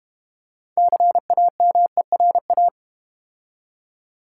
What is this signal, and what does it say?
Morse code, heard as a keyed tone, about 31 wpm, 700 Hz. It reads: CAMERA
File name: Easy Words - Speed-Racing with Hints - 25wpm